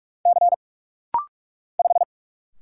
Morse code, heard as a keyed tone, about 45 wpm, 700 Hz. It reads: C 5
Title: Single Letter-Number Alt - Rapid-Fire 45wpm